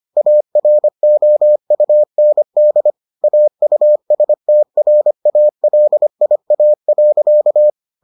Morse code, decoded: AROUND AUSTRALIA.

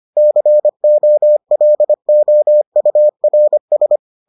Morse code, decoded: COLOURS

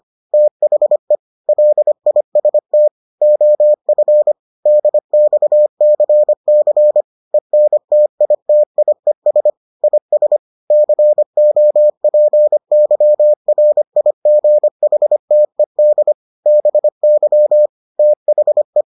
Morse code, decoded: THE LIST OF DXCC ENTITIES IS COPYRIGHTED BY THE

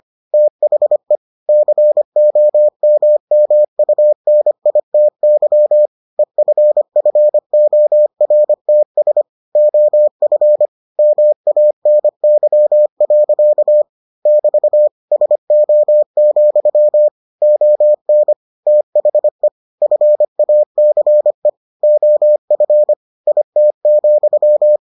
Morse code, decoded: THE COMMUNITY EFFORTS OF MANY. <BT> SO, ON THE FACE OF IT,

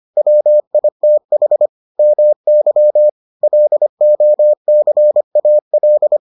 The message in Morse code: WITH MY LOCAL

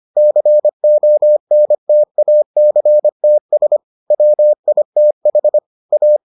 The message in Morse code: CONTACTS WITH A

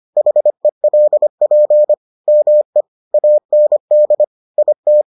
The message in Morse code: HELP ME AND IT

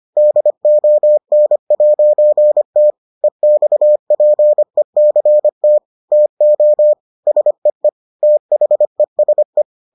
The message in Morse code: DON'T EXPECT TO SEE THESE

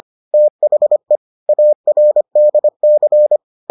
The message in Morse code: THE ARDC

